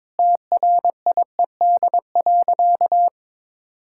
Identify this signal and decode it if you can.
Morse code, heard as a keyed tone, about 21 wpm, 700 Hz. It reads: TRIED.
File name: practice / 22wpm / Sentences with 8-Word Phrases - 22wpm